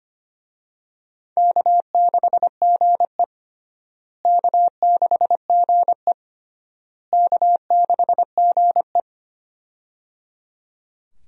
Morse code, heard as a keyed tone, about 25 wpm, 700 Hz. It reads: K6GE K6GE K6GE
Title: USA Call Signs - 4 Characters - Repeated 3x - 25wpm